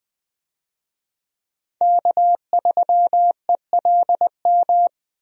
Morse code, heard as a keyed tone, about 20 wpm, 700 Hz. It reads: K3ELM
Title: USA Call Signs - 5 Characters - Repeated 2x - 20wpm